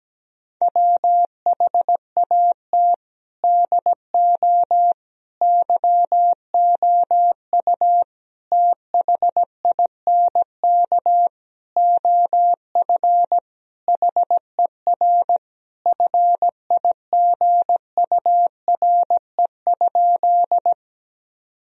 Morse code, heard as a keyed tone, about 17 wpm, 700 Hz. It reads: WHAT DO YOU THINK OF HER FIGURE?
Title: Sentences from Top 1000 Words - Review Part 2 - 17wpm